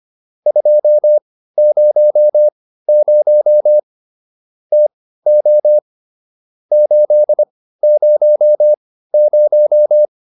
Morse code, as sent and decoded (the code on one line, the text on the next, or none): ..--- ----- ----- / - --- / ---.. ----- -----
200 TO 800